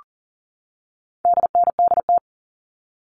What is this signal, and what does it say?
Morse code, heard as a keyed tone, about 40 wpm, 700 Hz. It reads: DNDT
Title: All Cut Numbers - 4-Digit 40wpm